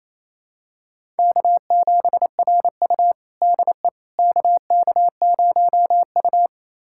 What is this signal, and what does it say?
Morse code, heard as a keyed tone, about 28 wpm, 700 Hz. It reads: K7RU DE KK0U
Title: USA Callsign DE Callsign - 4 Characters - 28wpm